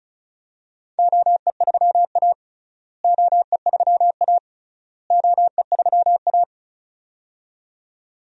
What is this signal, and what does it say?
Morse code, heard as a keyed tone, about 36 wpm, 700 Hz. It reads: OE3A OE3A OE3A
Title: DXCC-100 Call Signs - 4 Characters - Repeated 3x - 35wpm